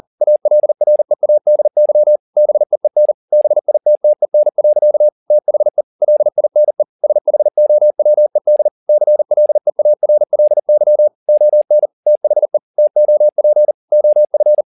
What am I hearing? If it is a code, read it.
Morse code, heard as a keyed tone, about 40 wpm, 600 Hz. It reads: ALREADY BEEN BITTEN. THE LINE SHOWED CLEARLY ON THE TOP OF
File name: the old man and the sea 2 40wpm